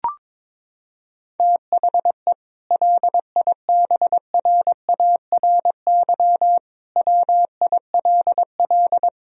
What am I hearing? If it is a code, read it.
Morse code, heard as a keyed tone, about 21 wpm, 700 Hz. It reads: THE LIBRARY WILL
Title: Sentences from Top 1000 Words - Review Part 4 - 22wpm